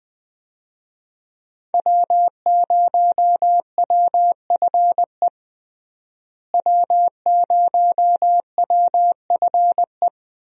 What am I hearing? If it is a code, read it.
Morse code, heard as a keyed tone, about 20 wpm, 700 Hz. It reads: W0WFE W0WFE
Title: USA Call Signs - 5 Characters - Repeated 3x - 20wpm